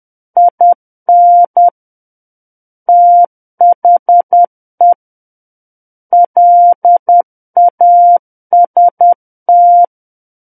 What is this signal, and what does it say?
Morse code, heard as a keyed tone, about 10 wpm, 700 Hz. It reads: IN THE LAST
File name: Book One / 01-Chapter 1 - 10 WPM